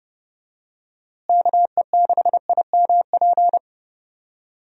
Morse code, heard as a keyed tone, about 31 wpm, 700 Hz. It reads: KE6IMP
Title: USA Call Signs - 6 Characters - Repeated 3x - 30wpm